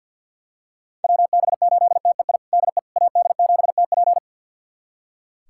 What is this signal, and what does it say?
Morse code, heard as a keyed tone, about 46 wpm, 700 Hz. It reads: WB8TEI DE AD7TP
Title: USA Callsign DE Callsign 50wpm